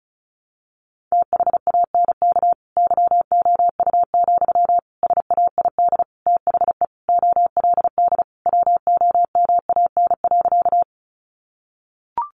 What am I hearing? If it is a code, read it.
Morse code, heard as a keyed tone, about 34 wpm, 700 Hz. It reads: THANK YOU, SAID THE OLD WOMAN.
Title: Sentences from Top 1100 Words 35wpm